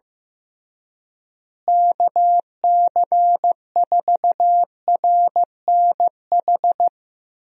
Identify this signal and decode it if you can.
Morse code, heard as a keyed tone, about 15 wpm, 700 Hz. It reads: KC4RNH
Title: USA Call Signs - 6 Characters - Repeated 3x - 15wpm